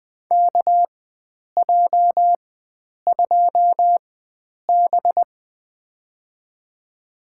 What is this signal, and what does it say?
Morse code, heard as a keyed tone, about 20 wpm, spaced as for 9 wpm, 700 Hz. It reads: KJ2B
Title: USA Call Signs - 4 Characters - 4x Character Spacing - 20wpm